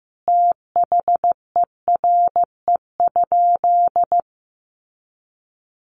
Morse code, heard as a keyed tone, about 15 wpm, 700 Hz. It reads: THERE?
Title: Sentences from Top 1300 Words 15wpm